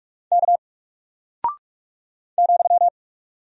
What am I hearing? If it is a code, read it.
Morse code, heard as a keyed tone, about 44 wpm, 700 Hz. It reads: K ,
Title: Single Character (Letter:Number:Punctuation:Prosign) Alt 45wpm